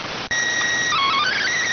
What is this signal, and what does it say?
SSTV (slow-scan television), PD 290.